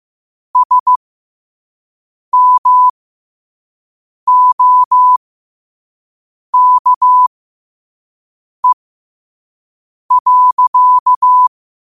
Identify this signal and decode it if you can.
Morse code, heard as a keyed tone, about 15 wpm, spaced as for 5 wpm, 1000 Hz. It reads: SMOKE.
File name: qotdfiles / 5wpm